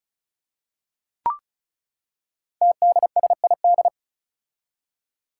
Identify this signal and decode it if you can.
Morse code, heard as a keyed tone, about 36 wpm, 700 Hz. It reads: TDSID
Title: Five Challenging Letter-Number Combinations 35wpm